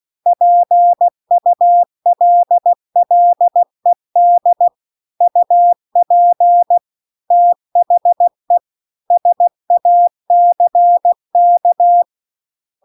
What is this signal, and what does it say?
Morse code, heard as a keyed tone, about 16 wpm, 700 Hz. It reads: PULLED UP THE SACK
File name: CW-The-Big-Twoheaded-River-2-16WPM